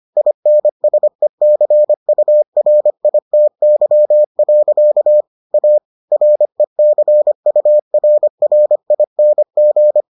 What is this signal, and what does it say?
Morse code, heard as a keyed tone, about 25 wpm, 600 Hz. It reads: INSECURITY. A RECURRING